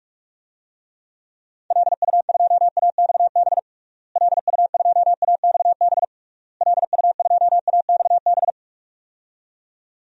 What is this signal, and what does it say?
Morse code, heard as a keyed tone, about 41 wpm, 700 Hz. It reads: LU2AXB LU2AXB LU2AXB
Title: DXCC-100 Call Signs - 6 Characters - Repeated 3x - 45wpm